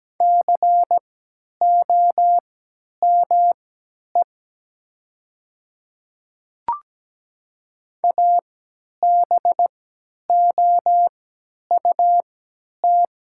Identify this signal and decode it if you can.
Morse code, heard as a keyed tone, about 17 wpm, spaced as for 10 wpm, 700 Hz. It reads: COME ABOUT
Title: 3x Character Spacing - Top 100 Words - 17wpm